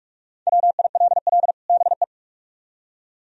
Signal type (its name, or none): Morse code